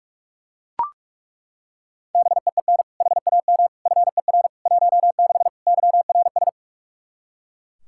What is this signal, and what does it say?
Morse code, heard as a keyed tone, about 41 wpm, 700 Hz. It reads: BEEN HAM FER 16 YRS
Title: Practice QSO - Exchange 3 - More Info - 45wpm